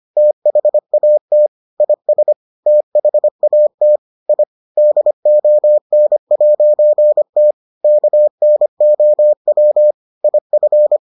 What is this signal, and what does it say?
Morse code, heard as a keyed tone, about 25 wpm, 600 Hz. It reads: THAT IS THAT I DON'T KNOW IF